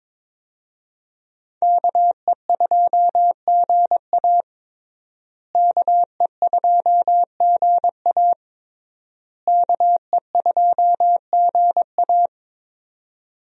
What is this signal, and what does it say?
Morse code, heard as a keyed tone, about 22 wpm, 700 Hz. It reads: KE2GA KE2GA KE2GA
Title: USA Call Signs - 5 Characters - Repeated 3x - 22wpm